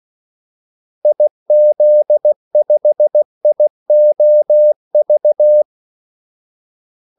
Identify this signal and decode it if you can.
Morse code, heard as a keyed tone, about 16 wpm, 600 Hz. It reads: IZ5IOV
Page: Kaldesignaler 16 wpm